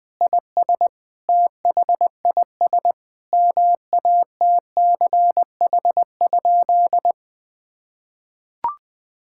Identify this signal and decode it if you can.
Morse code, heard as a keyed tone, about 20 wpm, 700 Hz. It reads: IS THIS MATCH?
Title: Sentences from Top 1600 Words 20wpm